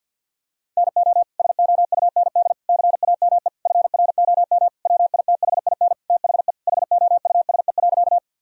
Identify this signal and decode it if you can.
Morse code, heard as a keyed tone, about 50 wpm, 700 Hz. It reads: NO SOUND CAME FROM WITHIN THE HOUSE.
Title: Sentences from Top 1000 Words - Review Part 4 - 50wpm